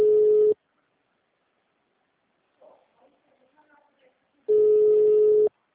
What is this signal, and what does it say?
Ringback tone (Europe, 425 Hz), with speech.
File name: out- -2033- - - .14473